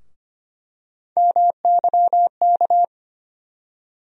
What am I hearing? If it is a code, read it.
Morse code, heard as a keyed tone, about 25 wpm, 700 Hz. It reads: MYK
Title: USA Call Sign Postfixes - 3 Letters 25wpm